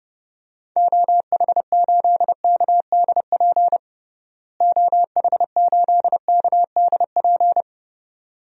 Morse code, heard as a keyed tone, about 30 wpm, 700 Hz. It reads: OH8KDP OH8KDP
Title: DXCC-100 Call Signs - 6 Characters - Repeated 2x - 30wpm